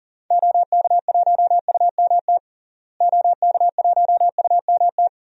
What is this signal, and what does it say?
Morse code, heard as a keyed tone, about 40 wpm, 700 Hz. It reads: OK1UMT OK1UMT
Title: DXCC-100 Call Signs - 6 Characters - Repeated 3x - 40wpm